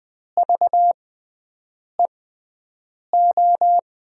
Morse code, heard as a keyed tone, about 20 wpm, spaced as for 7 wpm, 700 Hz. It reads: VEO
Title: USA Call Signs - 6x Character Spacing - 20wpm